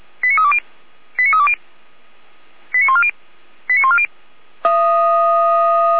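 Five-tone selective call (ZVEI), heard as quick runs of five tones, tones of about 70 ms. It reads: 89320 89320 89140 89140
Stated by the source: ZVEI five-tone call 89280 received on SDR.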